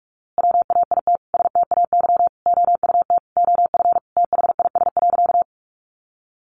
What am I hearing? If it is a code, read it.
Morse code, heard as a keyed tone, about 44 wpm, 700 Hz. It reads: WAIT STAY OUT OF THIS.
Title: A Sound Of Thunder 45wpm